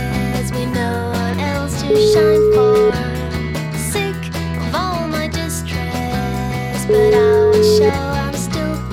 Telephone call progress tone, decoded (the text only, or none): ringback tone